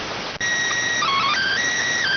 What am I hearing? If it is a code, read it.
SSTV (slow-scan television), PD 290.